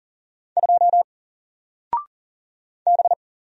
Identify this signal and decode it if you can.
Morse code, heard as a keyed tone, about 40 wpm, 700 Hz. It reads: 2 B
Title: Single Letter-Number 40wpm